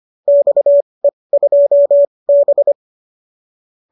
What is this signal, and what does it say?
Morse code, heard as a keyed tone, about 25 wpm, spaced as for 20 wpm, 590 Hz. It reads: XE2B